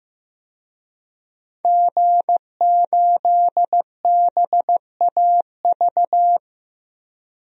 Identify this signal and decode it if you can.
Morse code, heard as a keyed tone, about 15 wpm, 700 Hz. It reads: G8BAV
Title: DXCC-100 Call Signs - 5 Characters - 15wpm